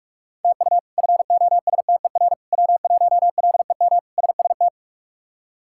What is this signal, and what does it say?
Morse code, heard as a keyed tone, about 41 wpm, 700 Hz. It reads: TU FOSTER W1LEM SST
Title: K1USN SST QSO Practice - 45wpm